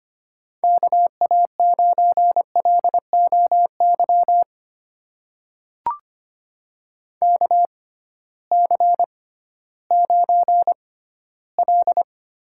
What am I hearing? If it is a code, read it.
Morse code, heard as a keyed tone, about 25 wpm, spaced as for 15 wpm, 700 Hz. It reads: KA9LOY KC9L
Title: USA Call Signs - 6 Characters - 6x Character Spacing - 25wpm